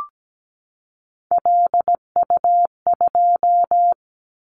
Morse code, heard as a keyed tone, about 17 wpm, 700 Hz. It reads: LU2
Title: Callsign Prefixes for Top 50 DXCC - 17wpm